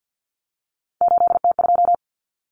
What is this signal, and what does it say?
Morse code, heard as a keyed tone, about 50 wpm, 700 Hz. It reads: 8T2
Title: Common Cut Numbers - 3-Digit 50wpm